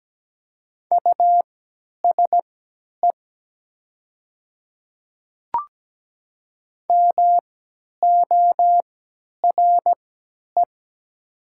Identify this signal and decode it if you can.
Morse code, heard as a keyed tone, about 17 wpm, spaced as for 10 wpm, 700 Hz. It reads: USE MORE
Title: 3x Character Spacing - Top 500 Words - 17wpm